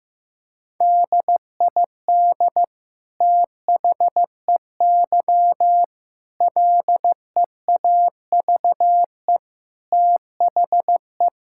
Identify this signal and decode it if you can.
Morse code, heard as a keyed tone, about 16 wpm, 700 Hz. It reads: DID THEY LEAVE THE
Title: Sentences from Top 1000 Words - Review Part 5 - 15wpm